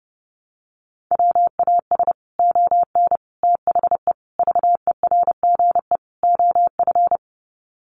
Morse code, heard as a keyed tone, about 30 wpm, 700 Hz. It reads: WAS ON THE VERGE OF
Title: Sets of 5 Words Encore 30wpm